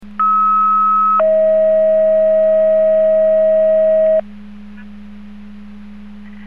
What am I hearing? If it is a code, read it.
Two-tone page, heard as a one-second tone, then a three-second tone, about 1290 then 640 Hz. It is heard from 0.2 to 4.2 s.
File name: Portage 154.385 Paging